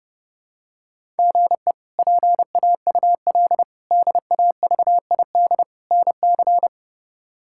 Morse code, heard as a keyed tone, about 30 wpm, 700 Hz. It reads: GE PAUL DAVID NC
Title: K1USN SST QSO Practice - 30wpm